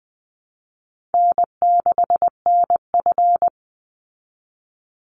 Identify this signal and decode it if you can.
Morse code, heard as a keyed tone, about 20 wpm, 700 Hz. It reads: N6NF
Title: Call Signs Repeated 2x - 20wpm